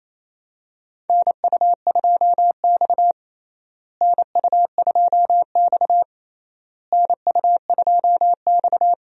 Morse code, heard as a keyed tone, about 28 wpm, 700 Hz. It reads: NU2X NU2X NU2X
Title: USA Call Signs - 4 Characters - Repeated 3x - 28wpm